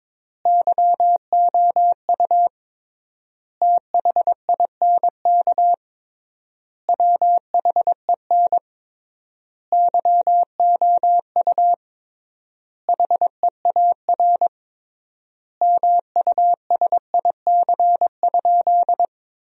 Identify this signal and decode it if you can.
Morse code, heard as a keyed tone, about 22 wpm, 700 Hz. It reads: YOU THINK WHEN YOU HEAR MUSIC?
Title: 3x Word-Spacing - Sentences from Top 300 Words - 22wpm